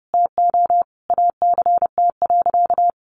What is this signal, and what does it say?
Morse code, heard as a keyed tone, about 28 wpm, 700 Hz. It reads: TO ACT.
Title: Sentences from Top 200 Words 30wpm